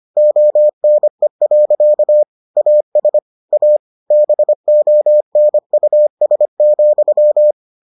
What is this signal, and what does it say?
Morse code, heard as a keyed tone, about 25 wpm, 600 Hz. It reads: ONE. AS A BONUS,